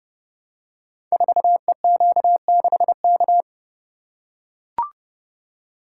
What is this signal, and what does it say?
Morse code, heard as a keyed tone, about 30 wpm, 700 Hz. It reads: 4EQ6K
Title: Code Groups - 5 Letter:Numbers - 30wpm